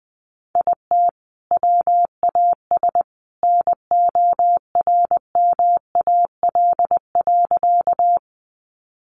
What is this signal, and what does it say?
Morse code, heard as a keyed tone, about 20 wpm, 700 Hz. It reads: IT WAS NORMAL.
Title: Sentences from Top 1200 Words 20wpm